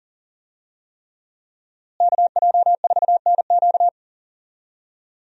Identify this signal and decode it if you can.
Morse code, heard as a keyed tone, about 40 wpm, 700 Hz. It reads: KJ4NQ
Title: USA Call Signs - 5 Characters - Repeated 3x - 40wpm